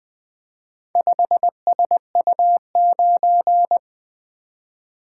Morse code, heard as a keyed tone, about 24 wpm, 700 Hz. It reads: 5SU9
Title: Speed-Racing - Four Letter-Number Combinations - 20wpm